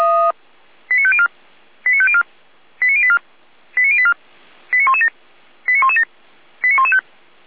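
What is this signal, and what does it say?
Five-tone selective call (ZVEI), heard as quick runs of five tones, tones of about 70 ms. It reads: 89504 89504 89084 89084 89117 89117 89115
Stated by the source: ZVEI five-tone selective call received via SDR.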